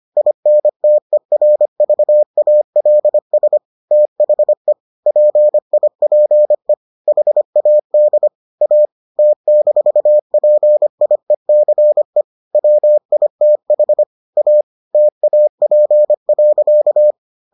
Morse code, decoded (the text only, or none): INTERVALS THE PIPE HAD A T-PIECE WITH A TAP.